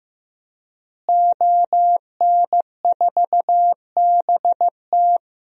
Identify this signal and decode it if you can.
Morse code, heard as a keyed tone, about 15 wpm, 700 Hz. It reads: ON4BT
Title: DXCC-100 Call Signs - 5 Characters - Repeated 2x - 15wpm